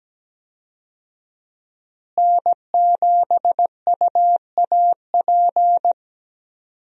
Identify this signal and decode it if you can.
Morse code, heard as a keyed tone, about 17 wpm, 700 Hz. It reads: N7UAP
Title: USA Call Signs - 5 Characters - Repeated 3x - 17wpm